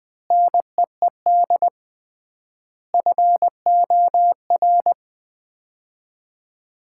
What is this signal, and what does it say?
Morse code, heard as a keyed tone, about 21 wpm, 700 Hz. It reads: NEED FOR
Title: 3x Word-Spacing - Sets of 3 Words - 20wpm